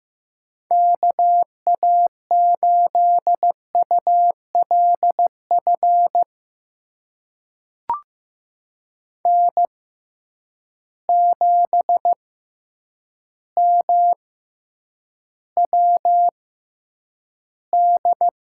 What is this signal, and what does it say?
Morse code, heard as a keyed tone, about 15 wpm, spaced as for 8 wpm, 700 Hz. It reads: KA8ULF N7MWD
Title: USA Call Signs - 6x Character Spacing - 15wpm